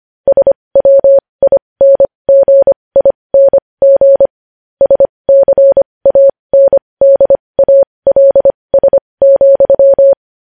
Morse code, heard as a keyed tone, about 25 wpm, spaced as for 20 wpm, 550 Hz. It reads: SWINGING SCANDALS,
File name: podcast / News headlines-1x-25x20